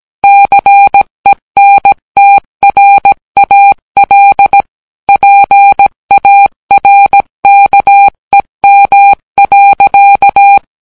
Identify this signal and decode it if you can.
Morse code, heard as a keyed tone, about 17 wpm, 800 Hz. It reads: CENTRAL PARKEM.